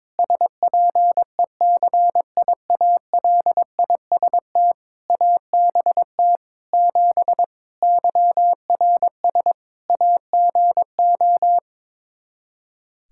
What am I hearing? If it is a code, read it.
Morse code, heard as a keyed tone, about 22 wpm, 700 Hz. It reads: SPECIALIST ABT 7 YRS AGO
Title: Practice QSO - Exchange 3 - More Info - 22wpm